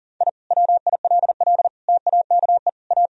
Morse code, heard as a keyed tone, about 40 wpm, 700 Hz. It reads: I WILL TAKE A
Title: Sentences from Top 1000 Words - Review Part 1 - 40wpm